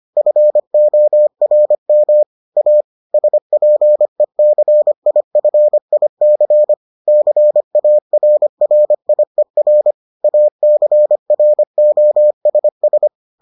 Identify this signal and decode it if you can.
Morse code, heard as a keyed tone, about 25 wpm, 600 Hz. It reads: FORM A SPECIFIC CARRIER ACROSS